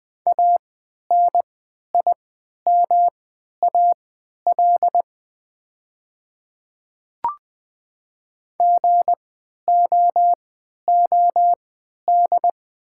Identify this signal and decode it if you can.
Morse code, heard as a keyed tone, about 20 wpm, spaced as for 11 wpm, 700 Hz. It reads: ANIMAL GOOD
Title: 3x Character Spacing - Top 300 Words - 20wpm